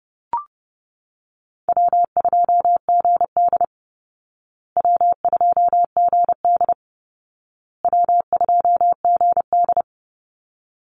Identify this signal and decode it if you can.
Morse code, heard as a keyed tone, about 30 wpm, 700 Hz. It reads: W2GD W2GD W2GD
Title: Easy Call Signs Repeated 3x - 30wpm